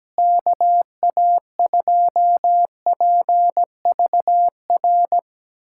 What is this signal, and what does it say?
Morse code, heard as a keyed tone, about 17 wpm, 700 Hz. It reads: KA2PVR
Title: USA Callsign DE Callsign - 6 Characters - 17wpm